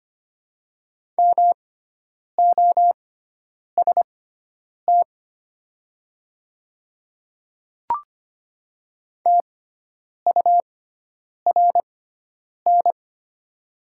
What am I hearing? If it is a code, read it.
Morse code, heard as a keyed tone, about 25 wpm, spaced as for 9 wpm, 700 Hz. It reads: MOST TURN
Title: 6x Character Spacing - Top 200 Words - 25wpm